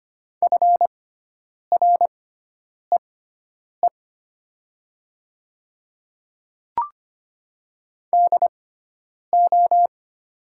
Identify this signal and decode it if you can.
Morse code, heard as a keyed tone, about 25 wpm, spaced as for 9 wpm, 700 Hz. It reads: FREE DO
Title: 6x Character Spacing - Top 500 Words - 25wpm